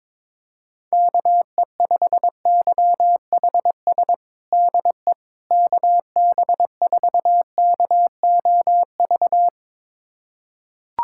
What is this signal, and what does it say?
Morse code, heard as a keyed tone, about 22 wpm, 700 Hz. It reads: KE5YHS DE KB4KOV
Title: USA Callsign DE Callsign - 6 Characters - 22wpm